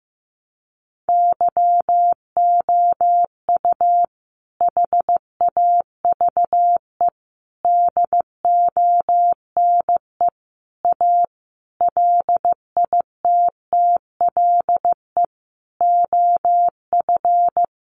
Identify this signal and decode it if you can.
Morse code, heard as a keyed tone, about 15 wpm, 700 Hz. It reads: YOU HAVE DONE A LITTLE OF
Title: Sentences from Top 700 Words 15wpm